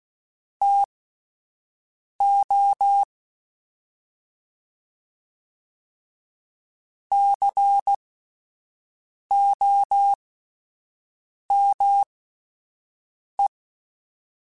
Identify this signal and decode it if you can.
Morse code, heard as a keyed tone, about 16 wpm, spaced as for 5 wpm, 800 Hz. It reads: TO COME
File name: (spaced) / QSOs Spaced-mp3s / WA7QOX 1 @10 sp